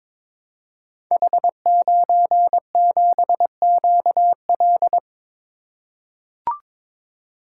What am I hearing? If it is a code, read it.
Morse code, heard as a keyed tone, about 22 wpm, 700 Hz. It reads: H97QL
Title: Code Groups - 5 Letter:Numbers - 22wpm